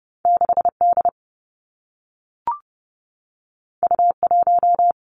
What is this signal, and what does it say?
Morse code, heard as a keyed tone, about 30 wpm, 700 Hz. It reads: 6 D U 1
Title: Two Challenging Letter-Number Combinations 30wpm